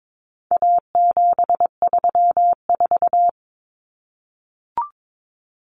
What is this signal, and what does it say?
Morse code, heard as a keyed tone, about 22 wpm, 700 Hz. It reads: A734
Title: Common Cut Numbers - 4-Digit 22wpm